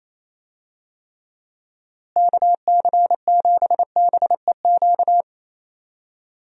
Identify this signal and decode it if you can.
Morse code, heard as a keyed tone, about 28 wpm, 700 Hz. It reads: KC7BEQ
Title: USA Call Signs - 6 Characters - Repeated 2x - 28wpm